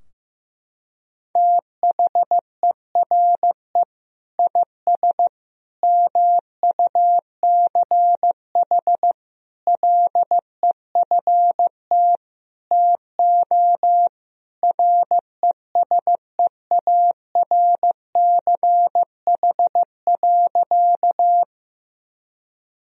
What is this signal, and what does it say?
Morse code, heard as a keyed tone, about 15 wpm, 700 Hz. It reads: THERE IS MUCH LEFT TO RESEARCH.
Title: Sentences from Top 1000 Words - Review Part 4 - 15wpm